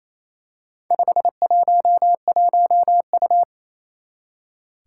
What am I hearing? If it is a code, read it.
Morse code, heard as a keyed tone, about 28 wpm, 700 Hz. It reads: 511U
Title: Four Challenging Letter-Number Combinations 28wpm